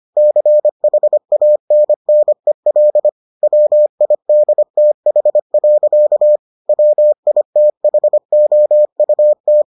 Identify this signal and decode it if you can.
Morse code, heard as a keyed tone, about 25 wpm, 600 Hz. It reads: CHANNEL WIDTH. WITHOUT